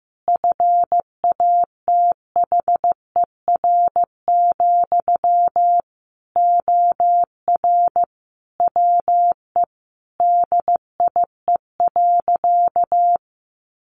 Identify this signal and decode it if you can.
Morse code, heard as a keyed tone, about 15 wpm, 700 Hz. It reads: FATHER, OR WE DIE.
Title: Sentences from Top 1300 Words 15wpm